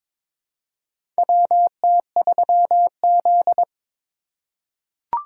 Morse code, heard as a keyed tone, about 22 wpm, 700 Hz. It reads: WT3Z
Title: USA Call Signs - 4 Characters - 2x Character Spacing - 22wpm